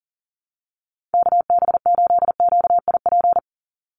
Morse code, heard as a keyed tone, about 40 wpm, 700 Hz. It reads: KB8QIP
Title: USA Call Signs 40wpm